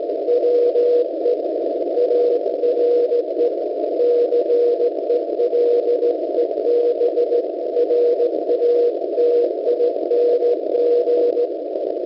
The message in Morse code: WE ARE CELEBRATING